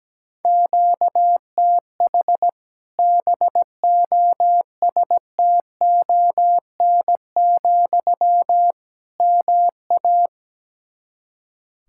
Morse code, heard as a keyed tone, about 17 wpm, 700 Hz. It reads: QTH BOSTON, MA
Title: Practice QSO - Exchange 1 - Beginning - 17wpm